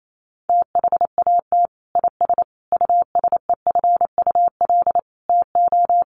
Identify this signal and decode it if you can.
Morse code, heard as a keyed tone, about 28 wpm, 700 Hz. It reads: THAT IS USEFUL TO
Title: Sentences from Top 700 Words 28wpm